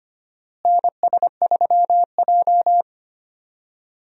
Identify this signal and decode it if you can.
Morse code, heard as a keyed tone, about 25 wpm, 700 Hz. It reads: NS3J
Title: USA Call Signs - 4 Characters - Repeated 3x - 25wpm